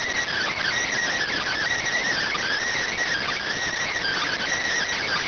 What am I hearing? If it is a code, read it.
Emergency Alert System (SAME) tones, heard as harsh buzzing bursts. It is heard from 0.3 to 4.4 s.